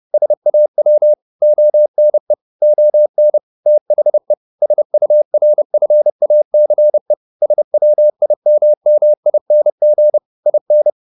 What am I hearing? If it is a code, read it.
Morse code, heard as a keyed tone, about 30 wpm, 600 Hz. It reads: SAW ONE ON THE SURFACE SWIMMING IN